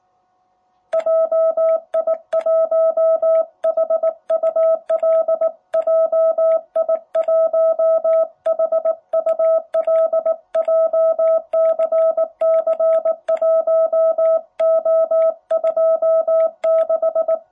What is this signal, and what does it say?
Morse code, heard as a keyed tone, about 19 wpm, 660 Hz. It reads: JI1HUL JI1HUL JCC1O26
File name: jh8cbh / 2021cw